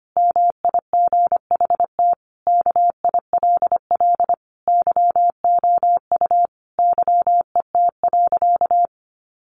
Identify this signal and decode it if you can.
Morse code, heard as a keyed tone, about 25 wpm, 700 Hz. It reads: MIGHT KILL YOU YET.
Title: A Sound Of Thunder 25wpm